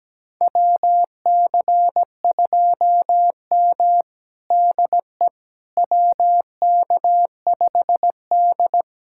Morse code, heard as a keyed tone, about 17 wpm, 700 Hz. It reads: WC2M DE WK5D
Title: USA Callsign DE Callsign - 4 Characters - 17wpm